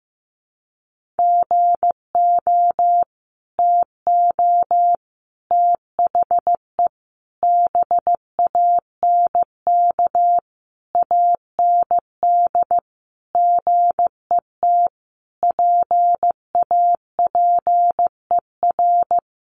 Morse code, heard as a keyed tone, about 15 wpm, 700 Hz. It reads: GO TO THE BANK AND GET PAPER
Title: Sentences from Top 1500 Words 15wpm